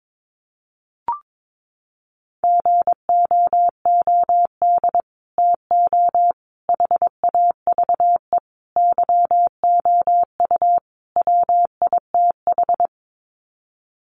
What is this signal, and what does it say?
Morse code, heard as a keyed tone, about 22 wpm, 700 Hz. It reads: GOOD TO HAVE YOU WITH
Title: Sets of 5 Words Encore2 22wpm